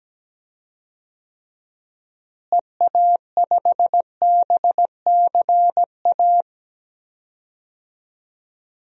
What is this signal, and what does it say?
Morse code, heard as a keyed tone, about 17 wpm, 700 Hz. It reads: EA5BCA
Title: DXCC-100 Call Signs - 6 Characters - Repeated 2x - 17wpm